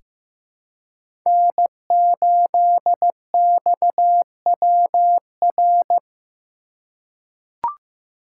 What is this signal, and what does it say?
Morse code, heard as a keyed tone, about 15 wpm, 700 Hz. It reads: N8XWR
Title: USA Call Signs - 2x Character Spacing - 15wpm